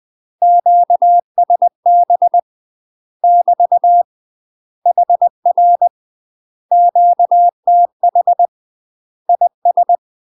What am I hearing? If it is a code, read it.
Morse code, heard as a keyed tone, about 20 wpm, 700 Hz. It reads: QSB <BT> HR QTH IS
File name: UK MASTER QSOs TRAINING / 20WPM / UK MASTER QSOs 0071